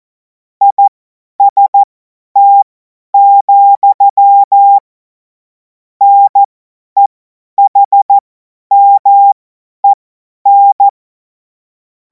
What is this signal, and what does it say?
Morse code, heard as a keyed tone, about 14 wpm, spaced as for 10 wpm, 800 Hz. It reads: IST, NEHMEN
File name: Morsekurs / Witze / Witz 03. März 2026 50 F70